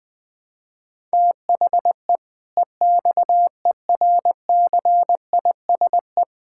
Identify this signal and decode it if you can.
Morse code, heard as a keyed tone, about 21 wpm, 700 Hz. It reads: THE EXERCISE
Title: Sentences from Top 1000 Words - Review Part 6 - 20wpm